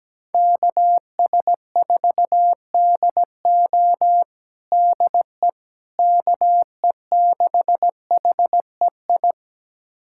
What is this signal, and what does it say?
Morse code, heard as a keyed tone, about 17 wpm, 700 Hz. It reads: KS4DO DE KE6HEI
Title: USA Callsign DE Callsign 17wpm